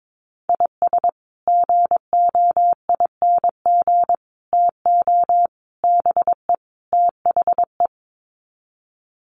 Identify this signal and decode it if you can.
Morse code, heard as a keyed tone, about 22 wpm, 700 Hz. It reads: IS GOING TO BE THE
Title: Sets of 5 Words 22wpm